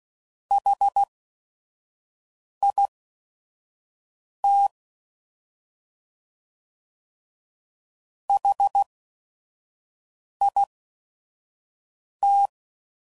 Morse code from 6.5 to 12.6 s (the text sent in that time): HIT